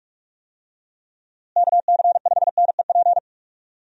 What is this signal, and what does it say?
Morse code, heard as a keyed tone, about 43 wpm, 700 Hz. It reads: KC5NEP
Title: USA Call Signs - 6 Characters - Repeated 2x - 45wpm